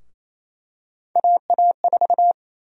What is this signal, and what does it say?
Morse code, heard as a keyed tone, about 27 wpm, 700 Hz. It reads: AA4
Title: USA Call Sign Prefixes - 28wpm